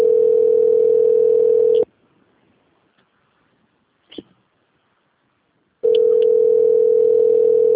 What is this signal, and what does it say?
Ringback tone (North America), 440 + 480 Hz.